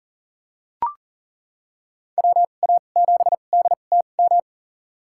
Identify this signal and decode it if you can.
Morse code, heard as a keyed tone, about 40 wpm, spaced as for 29 wpm, 700 Hz. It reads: WA7DTM
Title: USA Call Signs - 2x Character Spacing - 40wpm